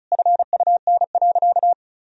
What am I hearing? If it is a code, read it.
Morse code, heard as a keyed tone, about 36 wpm, 700 Hz. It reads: FUN.
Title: Sentences from Top 1000 Words - Review Part 4 - 35wpm